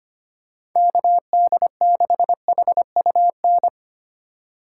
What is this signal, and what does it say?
Morse code, heard as a keyed tone, about 25 wpm, 700 Hz. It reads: KD6HUN
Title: USA Call Signs - 6 Characters - 2x Character Spacing - 25wpm